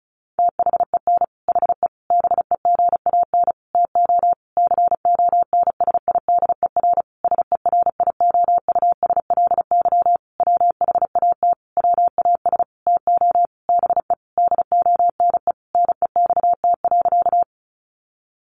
Morse code, heard as a keyed tone, about 34 wpm, 700 Hz. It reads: THEN HE BEGAN TO CONSIDER SERIOUSLY WHAT WAS TO BE DONE NEXT.
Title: Sentences from Top 1200 Words 35wpm